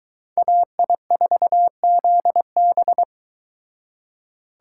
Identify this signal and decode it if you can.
Morse code, heard as a keyed tone, about 22 wpm, 700 Hz. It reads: AI4ZB